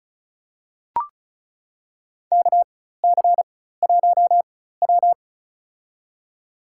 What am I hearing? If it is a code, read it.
Morse code, heard as a keyed tone, about 34 wpm, spaced as for 16 wpm, 700 Hz. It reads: KC1W
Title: USA Call Signs - 4 Characters - 4x Character Spacing - 35wpm